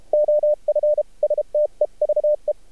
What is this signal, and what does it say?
Morse code, heard as a keyed tone, about 32 wpm, 600 Hz. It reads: OF STEVE
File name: archived / CRSnet